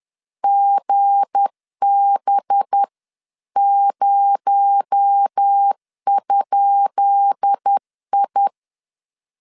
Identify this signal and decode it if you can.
Morse code, heard as a keyed tone, about 10 wpm, 800 Hz. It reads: GB0?I
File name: LEZIONI / 10 SIGNS / 10B INT 10wpm 2844